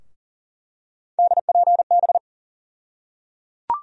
Morse code, heard as a keyed tone, about 38 wpm, 700 Hz. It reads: DPB
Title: USA Call Sign Postfixes - 3 Letters 40wpm